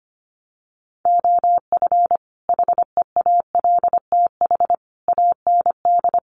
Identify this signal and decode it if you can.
Morse code, heard as a keyed tone, about 25 wpm, 700 Hz. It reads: OF HEALTH AND
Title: Sets of 5 Words 25wpm